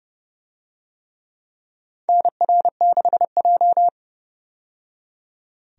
Morse code, heard as a keyed tone, about 30 wpm, 700 Hz. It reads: NR6J
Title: USA Call Signs - 4 Characters - 30wpm